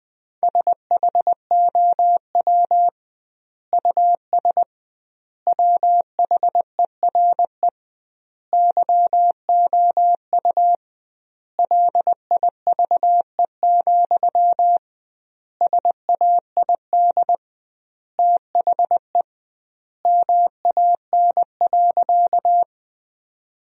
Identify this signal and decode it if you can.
Morse code, heard as a keyed tone, about 20 wpm, 700 Hz. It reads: SHOW US WHERE YOU LIVE, SAID THE MAN.
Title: 2x Word-Spacing - Sentences from Top 200 Words - 20wpm